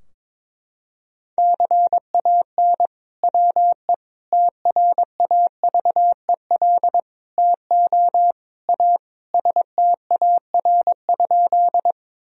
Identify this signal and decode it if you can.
Morse code, heard as a keyed tone, about 22 wpm, 700 Hz. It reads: CAN WE TRAVEL TO A STAR?
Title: Sentences from Top 1000 Words - Review Part 2 - 22wpm